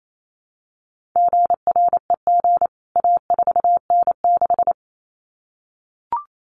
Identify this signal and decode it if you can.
Morse code, heard as a keyed tone, about 28 wpm, 700 Hz. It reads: GREG A4N6
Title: CWT Exchange 28wpm